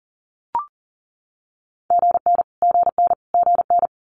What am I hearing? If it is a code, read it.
Morse code, heard as a keyed tone, about 42 wpm, 690 Hz. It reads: GN GN GN
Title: Multi-letter Phonemes Repeated 3x 40wpm